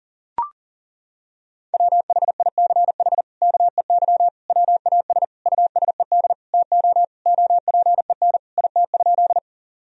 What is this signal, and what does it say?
Morse code, heard as a keyed tone, about 37 wpm, 700 Hz. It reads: WHICH KEY WAS USED TO OPEN IT?
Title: Sentences from Top 1000 Words - Review Part 4 - 40wpm